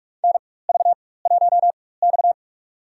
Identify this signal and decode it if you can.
Morse code, heard as a keyed tone, about 46 wpm, spaced as for 21 wpm, 700 Hz. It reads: NV1X
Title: USA Call Signs - 4 Characters - 4x Character Spacing - 45wpm